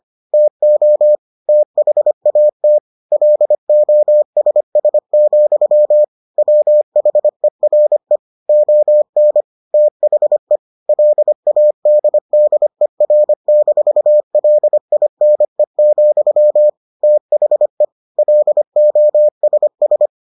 Morse code, heard as a keyed tone, about 25 wpm, 600 Hz. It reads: TO THAT LOSS, WHERE ON THE LADDER-LINE, THE LOSS